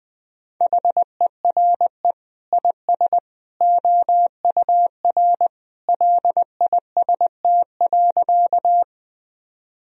Morse code, heard as a keyed tone, about 20 wpm, 700 Hz. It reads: HERE IS OUR LIST.
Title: Sentences from Top 1000 Words - Review Part 2 - 20wpm